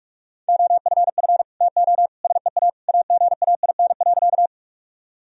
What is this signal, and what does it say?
Morse code, heard as a keyed tone, about 44 wpm, 700 Hz. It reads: OFF TO SEA AGAIN.
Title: Sentences from Top 1000 Words - Review Part 1 - 45wpm